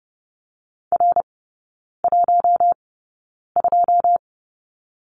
Morse code, heard as a keyed tone, about 30 wpm, spaced as for 9 wpm, 700 Hz. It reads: R12
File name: Lesson 16e - Letter c - Three Characters 30wpm